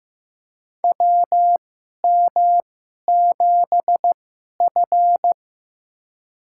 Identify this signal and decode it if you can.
Morse code, heard as a keyed tone, about 15 wpm, spaced as for 11 wpm, 700 Hz. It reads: WM7F
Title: USA Call Signs - 4 Characters - 2x Character Spacing - 15wpm